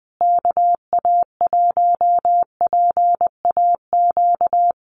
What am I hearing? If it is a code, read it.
Morse code, heard as a keyed tone, about 20 wpm, 700 Hz. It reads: KA1PAQ
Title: USA Call Signs Repeated 3x 20wpm